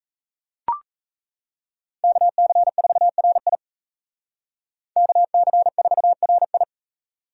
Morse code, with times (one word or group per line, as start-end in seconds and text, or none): 2.0-3.5 KC4RI
5.0-6.6 KC4RI